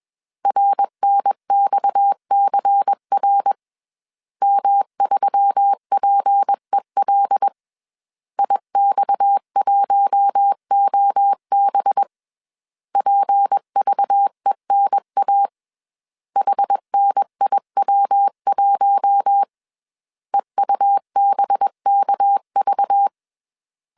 Morse code, heard as a keyed tone, about 21 wpm, 800 Hz. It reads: RNXCR M3PEL IX1OB PVENA HNIW1 EUBKV
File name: CW / DET / 1 6 1500C S B 20wpm 2048